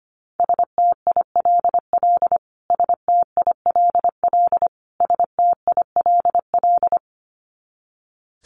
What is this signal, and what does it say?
Morse code, heard as a keyed tone, about 25 wpm, 700 Hz. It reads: STILL STILL STILL
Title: Top 500 Words Repeated 25wpm